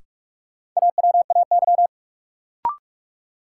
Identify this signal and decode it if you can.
Morse code, heard as a keyed tone, about 46 wpm, 700 Hz. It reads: AWAY
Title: Sentences from Top 600 Words - New Word Drill - 45wpm